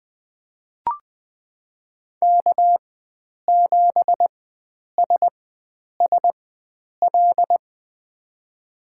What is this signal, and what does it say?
Morse code, heard as a keyed tone, about 20 wpm, spaced as for 9 wpm, 700 Hz. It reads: K7SSL
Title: USA Call Signs - 5 Characters - 4x Character Spacing - 20wpm